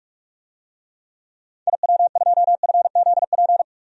Morse code, heard as a keyed tone, about 41 wpm, 700 Hz. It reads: IW2FZP
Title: DXCC-100 Call Signs - 6 Characters - 45wpm